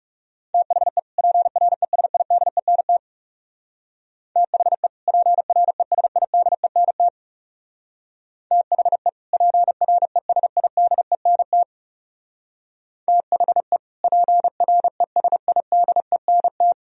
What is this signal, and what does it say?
Morse code, heard as a keyed tone, about 35 wpm, 700 Hz. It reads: THE PRESIDENT THE PRESIDENT THE PRESIDENT THE PRESIDENT
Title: Speed-Racing - Sets of 2 Words 30wpm